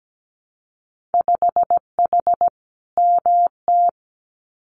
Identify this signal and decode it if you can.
Morse code, heard as a keyed tone, about 17 wpm, 700 Hz. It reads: 5H MT
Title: Winter Field Day Exchanges Encore 17wpm